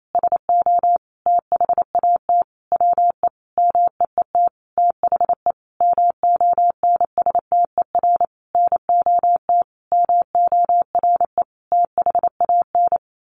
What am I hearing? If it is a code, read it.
Morse code, heard as a keyed tone, about 28 wpm, 700 Hz. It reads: SO THAT WE MEET THE MONSTER NOT MORE THAN
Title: A Sound Of Thunder 28wpm